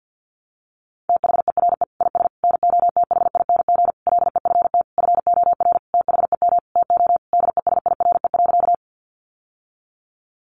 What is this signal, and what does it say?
Morse code, heard as a keyed tone, about 50 wpm, 700 Hz. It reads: THERE IS NOTHING LEFT FOR THEM TO DESIRE.
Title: Sentences from Top 1100 Words 50wpm